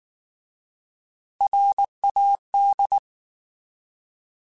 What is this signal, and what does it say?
Morse code, heard as a keyed tone, about 19 wpm, 800 Hz. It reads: RAD
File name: Alphabetical / Alphabetical mp3s / Rs @18